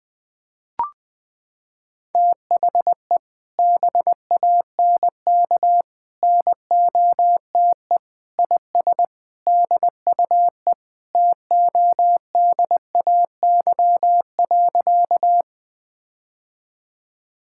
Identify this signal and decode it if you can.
Morse code, heard as a keyed tone, about 20 wpm, 700 Hz. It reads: THE BANK NOTE IS DUE TODAY.
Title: Sentences from Top 1000 Words - Review Part 6 - 20wpm